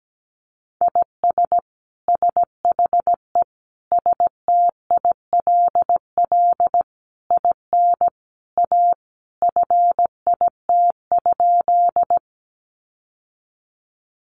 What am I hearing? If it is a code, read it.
Morse code, heard as a keyed tone, about 17 wpm, 700 Hz. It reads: IS SHE STILL IN A FIT?
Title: Sentences from Top 800 Words 17wpm